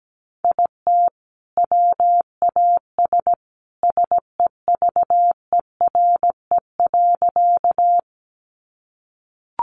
Morse code, heard as a keyed tone, about 17 wpm, 700 Hz. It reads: IT WAS SEVERE.
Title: Sentences from Top 1400 Words 17wpm